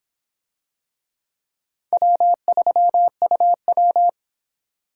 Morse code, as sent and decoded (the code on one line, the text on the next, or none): .-- ...-- ..- .--
W3UW